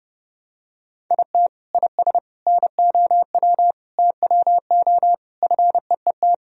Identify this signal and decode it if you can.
Morse code, heard as a keyed tone, about 29 wpm, 700 Hz. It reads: IT IS NOW TWO FEET
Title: Sentences from Top 1000 Words - Review Part 1 - 30wpm